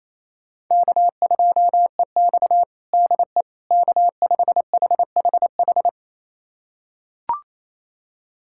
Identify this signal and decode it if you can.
Morse code, heard as a keyed tone, about 28 wpm, 700 Hz. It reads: K2EX DE K5HHH
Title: USA Callsign DE Callsign 28wpm